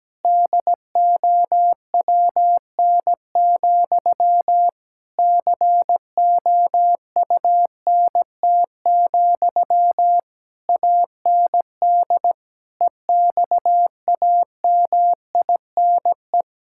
Morse code, heard as a keyed tone, about 17 wpm, 700 Hz. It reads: DOWN, COUNT, AND EXAMINE
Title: Sentences from Top 1600 Words 17wpm